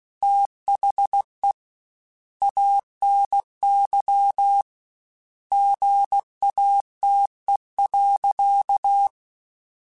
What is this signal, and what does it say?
Morse code, heard as a keyed tone, about 16 wpm, 800 Hz. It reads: THE ANY GATE.
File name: Technical Subjects / Technical Subjects-mp3s / OR Gate @15